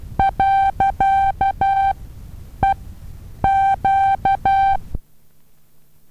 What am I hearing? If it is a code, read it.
Morse code, heard as a keyed tone, about 12 wpm, spaced as for 8 wpm, 780 Hz. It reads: .EQ
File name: Taviro HA6VA B